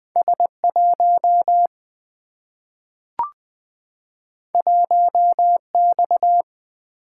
Morse code, heard as a keyed tone, about 20 wpm, 700 Hz. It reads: S 1 1 X
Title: Single-Letter Single-Number 20wpm